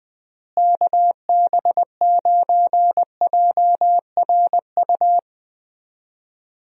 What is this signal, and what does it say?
Morse code, heard as a keyed tone, about 20 wpm, 700 Hz. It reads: KB9JRU
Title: USA Call Signs - 6 Characters - Repeated 2x - 20wpm